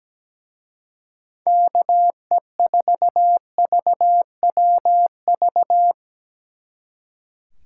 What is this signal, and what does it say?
Morse code, heard as a keyed tone, about 20 wpm, 700 Hz. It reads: KE4VWV
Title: Speed Racing - USA Call Signs 17wpm